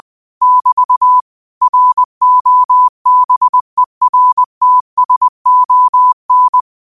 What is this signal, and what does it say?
Morse code, heard as a keyed tone, about 20 wpm, 1000 Hz. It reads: <BT> ROBERTSON